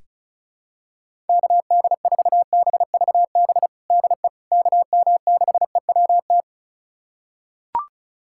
Morse code, heard as a keyed tone, about 34 wpm, 700 Hz. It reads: KD4BVB DE KM6EWT
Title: USA Callsign DE Callsign - 6 Characters - 35wpm